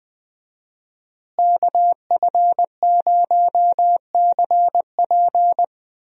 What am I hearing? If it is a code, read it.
Morse code, heard as a keyed tone, about 20 wpm, 700 Hz. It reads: KF0CP
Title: USA Call Signs - 5 Characters - Repeated 3x - 20wpm